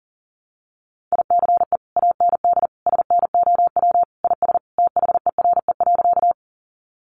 Morse code, heard as a keyed tone, about 40 wpm, 700 Hz. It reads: ICE AND SNOW IS THERE.
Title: Sentences from Top 1000 Words 40wpm